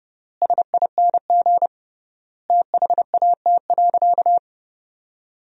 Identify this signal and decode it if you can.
Morse code, heard as a keyed tone, about 32 wpm, 700 Hz. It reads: SING THAT.
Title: 3x Word-Spacing - Sentences from Top 500 Words - 30wpm